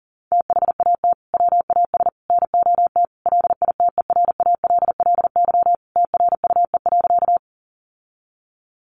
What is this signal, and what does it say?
Morse code, heard as a keyed tone, about 40 wpm, 700 Hz. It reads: THAT WAS NOT LITERALLY TRUE.
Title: Sentences from Top 1400 Words 40wpm